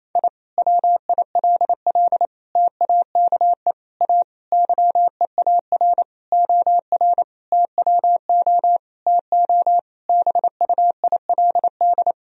Morse code, heard as a keyed tone, about 28 wpm, 700 Hz. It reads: I WILL TAKE A YEAR OR TWO TO BUILD
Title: Sentences from Top 1000 Words - Review Part 1 - 28wpm